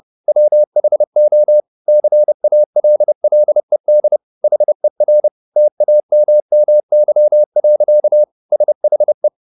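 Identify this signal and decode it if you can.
Morse code, heard as a keyed tone, about 30 wpm, 600 Hz. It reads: WHO CALLED HER TAMMY. SHE
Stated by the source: Morse code at 30 wpm, the novel Never by Ken Follett in English, from CW med Gnister.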